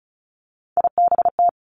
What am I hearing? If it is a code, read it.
Morse code, heard as a keyed tone, about 36 wpm, 700 Hz. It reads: IBT
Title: Three Challenging Letter-Number Combinations 35wpm